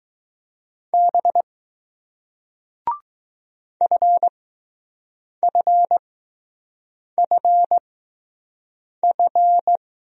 Morse code, heard as a keyed tone, about 20 wpm, spaced as for 6 wpm, 700 Hz. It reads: B FFFF